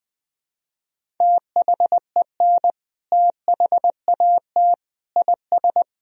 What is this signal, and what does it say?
Morse code, heard as a keyed tone, about 20 wpm, 700 Hz. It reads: THEN THAT IS
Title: Sentences from Top 1000 Words - Review Part 4 - 20wpm